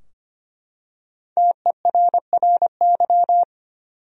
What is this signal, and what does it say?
Morse code, heard as a keyed tone, about 25 wpm, 700 Hz. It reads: TERRY
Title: USA Common Names - Max 5 Letters - 25wpm